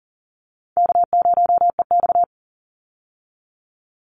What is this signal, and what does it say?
Morse code, heard as a keyed tone, about 40 wpm, 700 Hz. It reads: K0EX
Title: Call Signs Repeated 2x - 40wpm